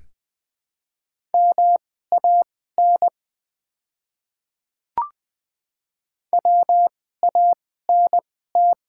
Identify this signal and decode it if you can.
Morse code, heard as a keyed tone, about 20 wpm, spaced as for 14 wpm, 700 Hz. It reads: MAN WANT
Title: 2x Character Spacing - Top 200 Words - 20wpm